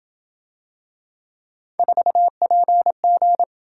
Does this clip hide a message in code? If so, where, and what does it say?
Yes: Morse code from 1.8 to 3.4 s, heard as a keyed tone, about 26 wpm, 700 Hz. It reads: 4PG